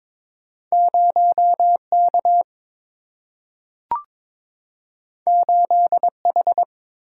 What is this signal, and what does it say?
Morse code, heard as a keyed tone, about 22 wpm, 700 Hz. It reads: 0K 8H
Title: Single-Letter Single-Number 22wpm